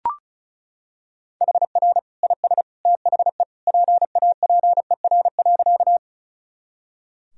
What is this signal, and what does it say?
Morse code, heard as a keyed tone, about 34 wpm, 700 Hz. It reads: HR IS THE PAPER.
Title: Sentences from Top 300 Words and Abbreviations 35wpm